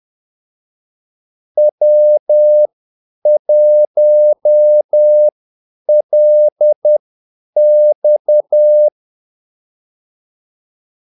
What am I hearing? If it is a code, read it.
Morse code, heard as a keyed tone, about 10 wpm, spaced as for 8 wpm, 600 Hz. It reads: W1LX
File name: Callsigns-10-8wpm